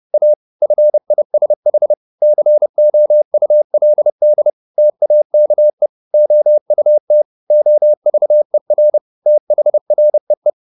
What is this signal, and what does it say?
Morse code, heard as a keyed tone, about 30 wpm, 600 Hz. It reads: A FISH COULD TAKE OUT OVER THREE
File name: the old man and the sea / the old man and the sea 1 30wpm